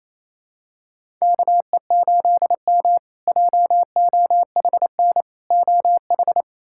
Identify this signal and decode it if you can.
Morse code, heard as a keyed tone, about 28 wpm, 700 Hz. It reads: KE8M JOHN OH
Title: K1USN SST QSO Practice - 28wpm